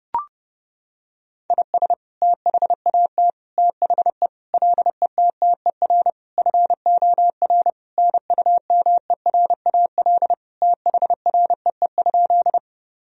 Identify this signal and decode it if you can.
Morse code, heard as a keyed tone, about 30 wpm, 700 Hz. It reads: IS THAT THE LETTER FOR NUMERAL THREE?
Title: Sentences from Top 1000 Words - Review Part 2 - 30wpm